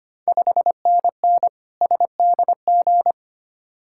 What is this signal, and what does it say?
Morse code, heard as a keyed tone, about 25 wpm, 690 Hz. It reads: 5NN SDG
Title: Ontario QSO Party Exchange 25wpm